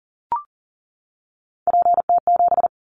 Morse code, heard as a keyed tone, about 39 wpm, 700 Hz. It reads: PT7
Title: Callsign Prefixes for Top 50 DXCC - 40wpm